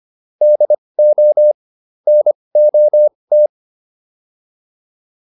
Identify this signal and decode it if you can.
Morse code, heard as a keyed tone, about 25 wpm, spaced as for 20 wpm, 600 Hz. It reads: DO NOT